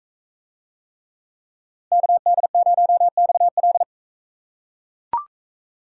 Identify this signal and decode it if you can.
Morse code, heard as a keyed tone, about 43 wpm, 700 Hz. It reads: KD0XL